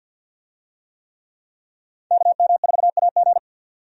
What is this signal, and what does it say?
Morse code, heard as a keyed tone, about 48 wpm, 700 Hz. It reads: KM4AG
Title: USA Call Signs - 5 Characters - 50wpm